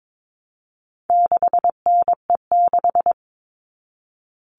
Morse code, heard as a keyed tone, about 22 wpm, 700 Hz. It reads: 6NE6
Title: All Cut Numbers - 4-Digit 22wpm